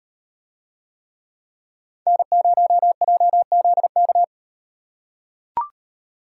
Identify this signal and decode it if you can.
Morse code, heard as a keyed tone, about 38 wpm, 700 Hz. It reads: N0JZK